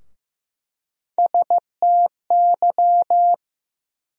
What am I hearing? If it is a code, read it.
Morse code, heard as a keyed tone, about 15 wpm, 700 Hz. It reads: STY
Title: USA Call Sign Postfixes - 3 Letters 15wpm